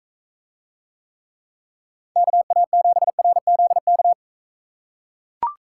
Morse code, heard as a keyed tone, about 44 wpm, 700 Hz. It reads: KA7RZK